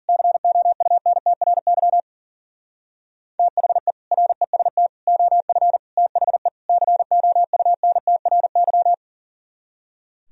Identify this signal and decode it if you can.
Morse code, heard as a keyed tone, about 50 wpm, 700 Hz. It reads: COUNTRY THE REST OF THE COUNTRY